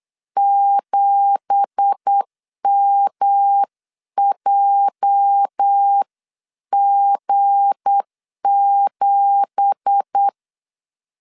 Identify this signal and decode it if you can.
Morse code, heard as a keyed tone, about 8 wpm, 800 Hz. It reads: 7MJG7
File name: 8wpm